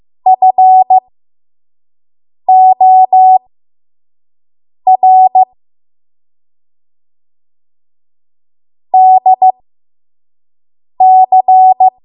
Morse code, heard as a keyed tone, about 15 wpm, spaced as for 5 wpm, 750 Hz. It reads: FOR DC
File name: Morse / 05WPM